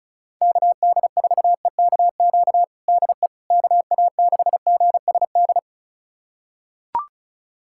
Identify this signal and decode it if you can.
Morse code, heard as a keyed tone, about 36 wpm, 700 Hz. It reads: KD4EKQ DE KA6GSD
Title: USA Callsign DE Callsign - 6 Characters - 35wpm